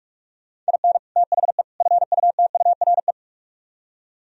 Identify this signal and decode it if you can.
Morse code, heard as a keyed tone, about 57 wpm, 700 Hz. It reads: IN THE FUTURE